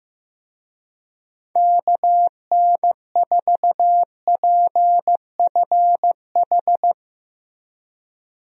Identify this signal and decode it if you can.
Morse code, heard as a keyed tone, about 15 wpm, 700 Hz. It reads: KN4PFH
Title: USA Call Signs - 6 Characters - Repeated 3x - 15wpm